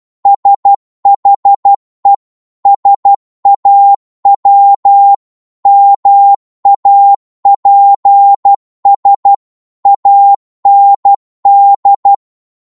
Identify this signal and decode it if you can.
Morse code, heard as a keyed tone, about 12 wpm, 800 Hz. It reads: SHE SAW MAPS AND
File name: cwmp3 / psg-1-12wpm